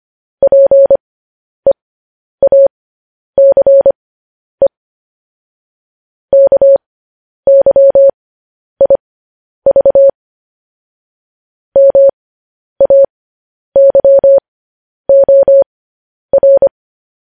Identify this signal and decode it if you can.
Morse code, heard as a keyed tone, about 25 wpm, spaced as for 10 wpm, 550 Hz. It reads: PEACE KYIV MAYOR
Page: news headlines in morse code at 10 wpm